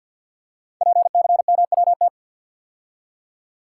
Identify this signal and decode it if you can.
Morse code, heard as a keyed tone, about 53 wpm, 700 Hz. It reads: PCMWT
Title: Code Groups - 5 Letters - 50wpm